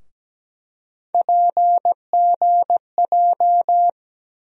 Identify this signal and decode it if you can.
Morse code, heard as a keyed tone, about 17 wpm, 700 Hz. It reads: PGJ
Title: USA Call Sign Postfixes - 3 Letters 17wpm